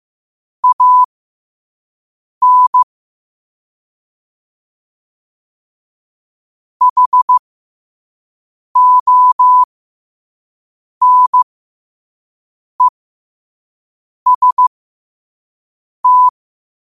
Morse code, heard as a keyed tone, about 15 wpm, spaced as for 5 wpm, 1000 Hz. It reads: AN HONEST
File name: qotdfiles / 5wpm